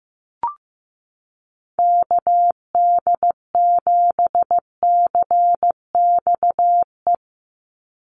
Morse code, heard as a keyed tone, about 15 wpm, 700 Hz. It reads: KD7CXE
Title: USA Call Signs 15wpm